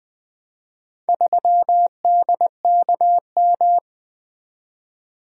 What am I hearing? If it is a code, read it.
Morse code, heard as a keyed tone, about 20 wpm, 700 Hz. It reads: 3DKM
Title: Four Letter-Number Combinations - 20wpm